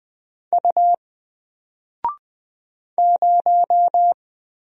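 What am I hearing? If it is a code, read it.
Morse code, heard as a keyed tone, about 20 wpm, 700 Hz. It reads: U 0
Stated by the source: Single letter and number Morse drill at 20 wpm, from N4LCD.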